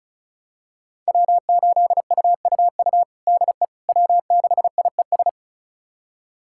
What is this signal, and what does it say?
Morse code, heard as a keyed tone, about 34 wpm, 700 Hz. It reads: W8UUU DE W6IES
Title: USA Callsign DE Callsign - 5 Characters - 35wpm